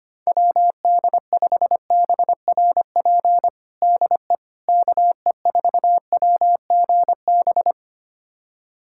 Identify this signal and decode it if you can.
Morse code, heard as a keyed tone, about 25 wpm, 700 Hz. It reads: WD5BRP DE KE4WGB
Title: USA Callsign DE Callsign 25wpm